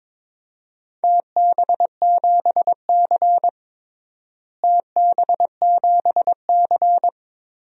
Morse code, heard as a keyed tone, about 22 wpm, 700 Hz. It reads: TB7C TB7C
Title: DXCC-100 Call Signs - 4 Characters - Repeated 3x - 22wpm